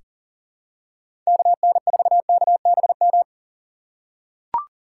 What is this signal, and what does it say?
Morse code, heard as a keyed tone, about 38 wpm, 700 Hz. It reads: KN4KBM
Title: USA Call Signs - 4x Character Spacing - 40wpm